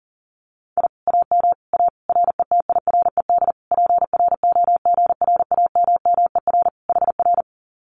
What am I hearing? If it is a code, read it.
Morse code, heard as a keyed tone, about 40 wpm, 700 Hz. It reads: I AM A RETIRED PROGRAMMER HR
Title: Common QSO Phrases Encore 40wpm